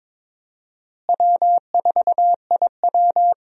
Morse code, heard as a keyed tone, about 22 wpm, 700 Hz. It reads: W4IW
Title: USA Call Signs - 4 Characters - Repeated 2x - 22wpm